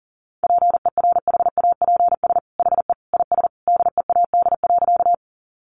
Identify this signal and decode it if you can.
Morse code, heard as a keyed tone, about 40 wpm, 700 Hz. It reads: PERHAPS HE IS DEAD.
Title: Sentences from Top 1500 Words 40wpm